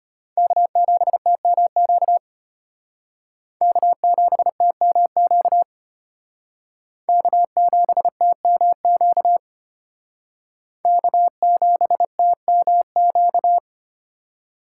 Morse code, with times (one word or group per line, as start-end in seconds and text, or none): 0.4-2.2 K7TMQ
3.6-5.6 K7TMQ
7.1-9.4 K7TMQ
10.9-13.6 K7TMQ